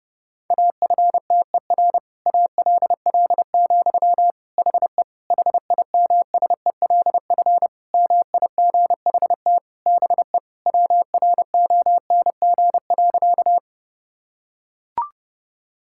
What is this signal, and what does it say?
Morse code, heard as a keyed tone, about 30 wpm, 700 Hz. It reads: AFTER ALL, HE HIMSELF MIGHT BE WRONG.
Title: Sentences from Top 1600 Words 30wpm